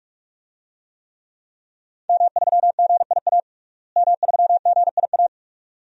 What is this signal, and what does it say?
Morse code, heard as a keyed tone, about 46 wpm, 700 Hz. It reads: M3GIA M3GIA
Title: DXCC-100 Call Signs - 5 Characters - Repeated 2x - 45wpm